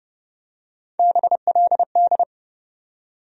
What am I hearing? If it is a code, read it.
Morse code, heard as a keyed tone, about 30 wpm, 700 Hz. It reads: BLD
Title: USA Call Sign Postfixes - 3 Letters 30wpm